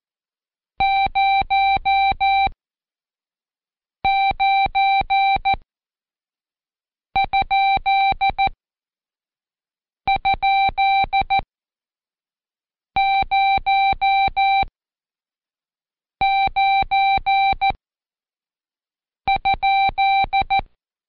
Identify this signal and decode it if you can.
Morse code, heard as a keyed tone, about 14 wpm, spaced as for 5 wpm, 780 Hz. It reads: 09??09?